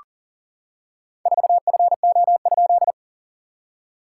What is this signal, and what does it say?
Morse code, heard as a keyed tone, about 40 wpm, 690 Hz. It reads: 4FO?
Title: SOTA QSO Practice 40wpm